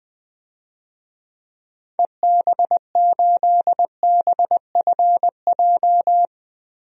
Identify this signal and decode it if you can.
Morse code, heard as a keyed tone, about 20 wpm, 700 Hz. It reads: EB8BFJ
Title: DXCC-100 Call Signs - 6 Characters - 20wpm